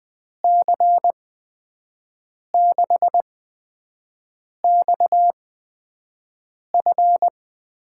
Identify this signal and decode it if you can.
Morse code, heard as a keyed tone, about 20 wpm, spaced as for 5 wpm, 700 Hz. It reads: C6XF
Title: USA Call Signs - 5 Characters - 8x Character Spacing - 20wpm